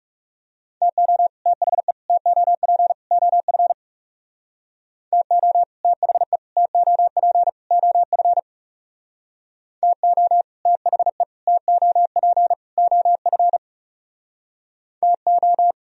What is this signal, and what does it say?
Morse code, heard as a keyed tone, about 35 wpm, 700 Hz. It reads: TO THE TOP OF TO THE TOP OF TO THE TOP OF TO
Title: Speed-Racing - Sets of 4 Words 30wpm